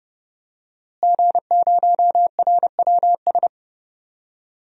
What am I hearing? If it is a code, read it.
Morse code, heard as a keyed tone, about 30 wpm, 700 Hz. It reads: G0RWS
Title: DXCC-100 Call Signs - 5 Characters - Repeated 2x - 30wpm